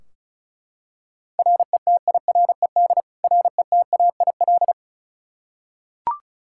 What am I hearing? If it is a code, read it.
Morse code, heard as a keyed tone, about 36 wpm, 700 Hz. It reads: RETIRED RETAIL
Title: K1USN SST QSO Practice 35wpm